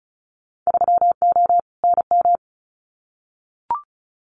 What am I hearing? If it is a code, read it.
Morse code, heard as a keyed tone, about 34 wpm, 700 Hz. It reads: 3O NM
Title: Winter Field Day Exchanges Encore 35wpm